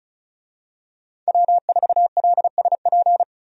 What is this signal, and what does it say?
Morse code, heard as a keyed tone, about 36 wpm, 700 Hz. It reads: W4LSP
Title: USA Call Signs - 5 Characters - 35wpm